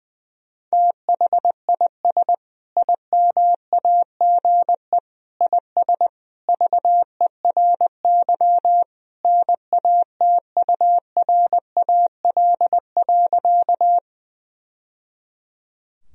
Morse code, heard as a keyed tone, about 21 wpm, 700 Hz. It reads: THIS IMAGE IS VERY NATURAL.
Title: Sentences from Top 1000 Words - Review Part 4 - 20wpm